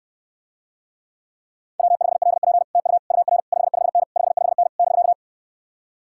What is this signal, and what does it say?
Morse code, heard as a keyed tone, about 68 wpm, 700 Hz. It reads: W5LP TU UR 55N 55N <BK>